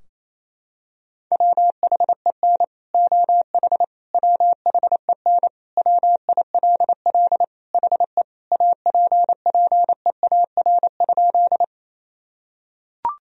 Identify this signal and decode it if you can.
Morse code, heard as a keyed tone, about 28 wpm, 700 Hz. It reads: WHEN OH WHEN WILL HE APPEAR?
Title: Sentences from Top 1000 Words - Review Part 2 - 28wpm